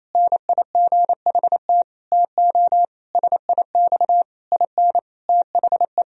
Morse code, heard as a keyed tone, about 27 wpm, 700 Hz. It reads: NIGHT TO SIX IN THE
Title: Sentences from Top 1000 Words - Review Part 2 - 28wpm